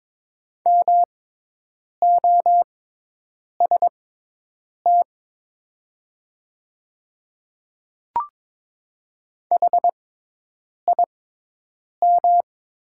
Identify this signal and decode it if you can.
Morse code, heard as a keyed tone, about 22 wpm, spaced as for 8 wpm, 700 Hz. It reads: MOST HIM
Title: 6x Character Spacing - Top 100 Words - 22wpm